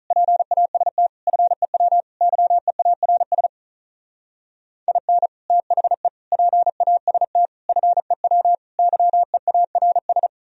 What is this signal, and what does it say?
Morse code, heard as a keyed tone, about 42 wpm, 700 Hz. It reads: PAST FEW YEARS IN THE PAST FEW YEARS
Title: Speed-Racing - Sets of 5 Words 35wpm